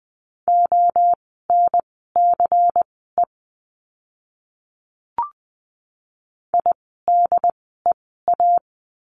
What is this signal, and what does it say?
Morse code, heard as a keyed tone, about 20 wpm, spaced as for 14 wpm, 700 Hz. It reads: ONCE IDEA
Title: 108 Lesson 18f - Letter m - Words 20wpm